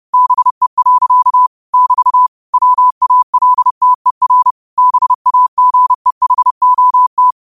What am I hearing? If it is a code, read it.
Morse code, heard as a keyed tone, about 30 wpm, 1000 Hz. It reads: BE. <BT> WALTER BAGEHOT
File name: qotdfiles / 30wpm